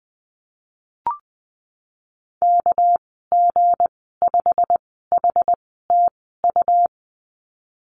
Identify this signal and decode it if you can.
Morse code, heard as a keyed tone, about 20 wpm, spaced as for 15 wpm, 700 Hz. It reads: KG5HTU
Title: 207 Lesson 33g - Letter j - Call Signs 20wpm